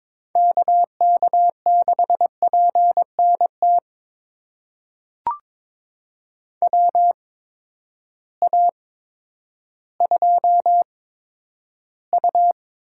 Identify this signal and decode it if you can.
Morse code, heard as a keyed tone, about 22 wpm, spaced as for 11 wpm, 700 Hz. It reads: KK6PNT WA2U
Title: USA Call Signs - 8x Character Spacing - 22wpm